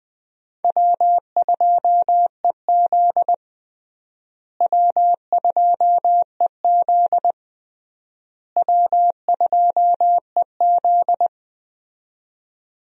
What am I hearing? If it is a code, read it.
Morse code, heard as a keyed tone, about 20 wpm, 700 Hz. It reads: W2EZ W2EZ W2EZ
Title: USA Call Signs - 4 Characters - Repeated 3x - 20wpm